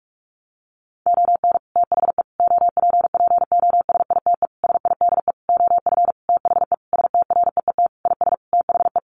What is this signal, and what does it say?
Morse code, heard as a keyed tone, about 44 wpm, 700 Hz. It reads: ON THE OPPOSITE SIDE OF THE STREET IS THE
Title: Sentences from Top 1400 Words 45wpm